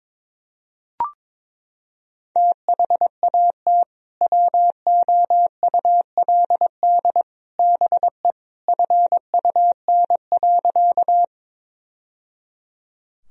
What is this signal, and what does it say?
Morse code, heard as a keyed tone, about 22 wpm, 700 Hz. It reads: THAT WOULD BE FUN.
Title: Sentences from Top 1000 Words - Review Part 4 - 22wpm